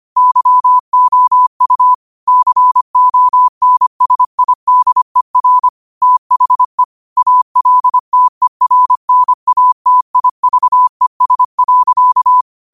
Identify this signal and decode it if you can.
Morse code, heard as a keyed tone, about 25 wpm, 1000 Hz. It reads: YOU CONSIDER THE ALTERNATIVES.
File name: qotdfiles / 25wpm